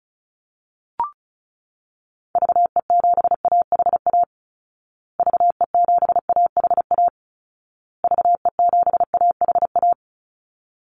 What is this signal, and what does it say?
Morse code, heard as a keyed tone, about 34 wpm, 700 Hz. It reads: VE7AHA VE7AHA VE7AHA
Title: Call Signs Repeated 2x - 35wpm